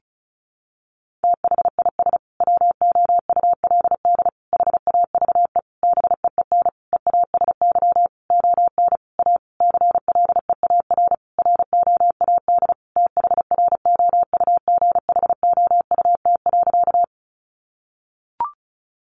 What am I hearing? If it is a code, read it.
Morse code, heard as a keyed tone, about 34 wpm, 700 Hz. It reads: THIS WOULD HAVE BEEN EASY ON A CLEAR ROAD THROUGHOUT.
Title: Sentences from Top 800 Words 35wpm